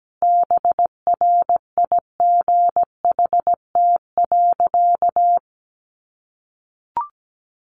Morse code, heard as a keyed tone, about 17 wpm, 700 Hz. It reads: BRIGHT.
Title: Sentences from Top 1500 Words 17wpm